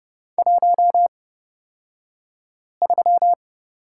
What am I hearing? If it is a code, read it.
Morse code, heard as a keyed tone, about 30 wpm, spaced as for 5 wpm, 700 Hz. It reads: 13
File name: Lesson 38d - Letter q - Two Characters 30wpm